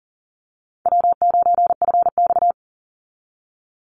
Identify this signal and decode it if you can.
Morse code, heard as a keyed tone, about 40 wpm, 700 Hz. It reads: W9FX
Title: Call Signs Repeated 3x - 40wpm